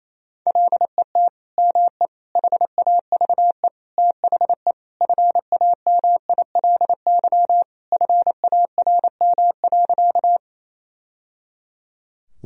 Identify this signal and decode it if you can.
Morse code, heard as a keyed tone, about 27 wpm, 700 Hz. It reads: LET ME HAVE THE FAMILY FARM.
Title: Sentences from Top 1000 Words - Review Part 2 - 28wpm